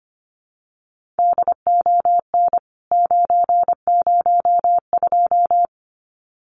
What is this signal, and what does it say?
Morse code, heard as a keyed tone, about 25 wpm, 700 Hz. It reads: DON 902
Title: CWT Exchange 25wpm